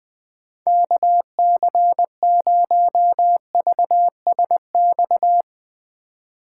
Morse code, heard as a keyed tone, about 21 wpm, 700 Hz. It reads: KC0VSX